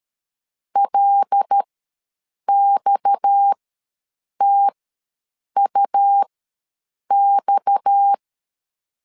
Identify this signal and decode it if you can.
Morse code, heard as a keyed tone, about 13 wpm, spaced as for 7 wpm, 800 Hz. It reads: LXTUX
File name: LEZIONI / 6 PKX0 / 6 PKX ws 12wpm 3106